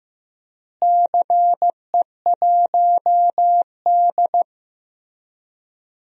Morse code, heard as a keyed tone, about 15 wpm, 700 Hz. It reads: CE1D
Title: DXCC-100 Call Signs - 4 Characters - Repeated 3x - 15wpm